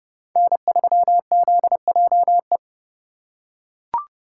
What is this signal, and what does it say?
Morse code, heard as a keyed tone, about 31 wpm, 700 Hz. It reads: N3ZJE
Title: USA Call Signs - 5 Characters - 2x Character Spacing - 30wpm